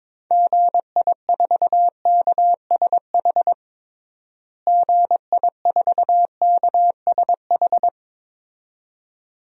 Morse code, heard as a keyed tone, about 22 wpm, 700 Hz. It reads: GI4KSH GI4KSH
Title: DXCC-100 Call Signs - 6 Characters - Repeated 2x - 22wpm